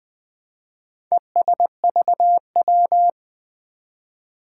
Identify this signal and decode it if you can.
Morse code, heard as a keyed tone, about 20 wpm, 700 Hz. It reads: ESVW
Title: Four Challenging Letter-Number Combinations 20wpm